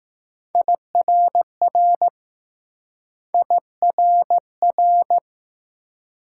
Morse code, heard as a keyed tone, about 20 wpm, 700 Hz. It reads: IRR IRR